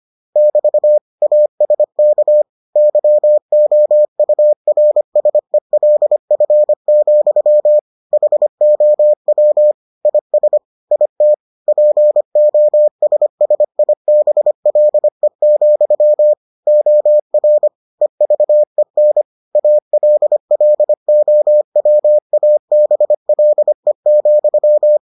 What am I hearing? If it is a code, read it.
Morse code, heard as a keyed tone, about 25 wpm, 600 Hz. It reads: <BT> ASK YOURSELF, HOW IS IT POSSIBLE, OR EVEN ALLOWABLE,